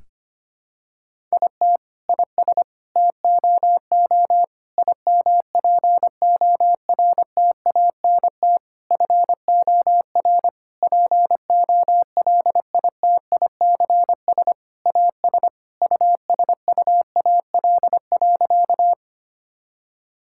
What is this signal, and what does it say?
Morse code, heard as a keyed tone, about 25 wpm, 700 Hz. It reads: IT IS TOO IMPORTANT FOR POLITICS AS USUAL.
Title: Sentences from Top 1000 Words - Review Part 6 - 25wpm